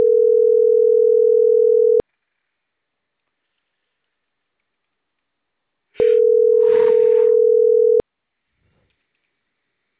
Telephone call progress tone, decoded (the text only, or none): ringback tone